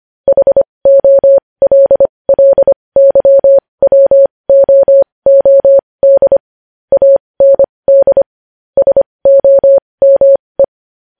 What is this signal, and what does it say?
Morse code, heard as a keyed tone, about 25 wpm, spaced as for 20 wpm, 550 Hz. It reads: HOLLYWOOD AND SOME
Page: news-headlines-in-morse-code-at-20-wpm